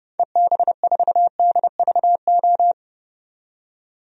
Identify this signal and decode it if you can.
Morse code, heard as a keyed tone, about 30 wpm, 700 Hz. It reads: EB4DVO
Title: DXCC-100 Call Signs - 6 Characters - 30wpm